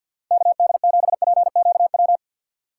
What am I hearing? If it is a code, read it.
Morse code, heard as a keyed tone, about 50 wpm, 700 Hz. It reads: KD7PQW
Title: USA Call Signs - 6 Characters - 3x Character Spacing - 50wpm